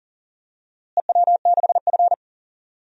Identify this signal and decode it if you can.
Morse code, heard as a keyed tone, about 37 wpm, 700 Hz. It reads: EW6F
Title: DXCC-100 Call Signs - 4 Characters - Repeated 3x - 40wpm